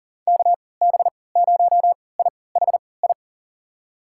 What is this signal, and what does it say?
Morse code, heard as a keyed tone, about 40 wpm, spaced as for 23 wpm, 700 Hz. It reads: KB0IHI
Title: USA Call Signs - 6 Characters - 3x Character Spacing - 40wpm